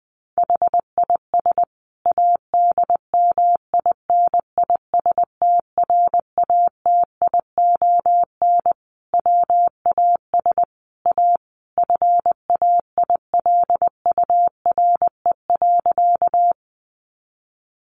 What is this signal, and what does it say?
Morse code, heard as a keyed tone, about 20 wpm, 700 Hz. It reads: HIS ADMINISTRATION WAS A FAILURE.
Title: Sentences from Top 1300 Words 20wpm